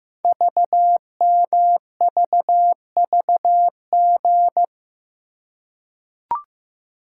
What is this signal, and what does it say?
Morse code, heard as a keyed tone, about 15 wpm, 700 Hz. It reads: VMVVG
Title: Code Groups - 5 Letters - 15wpm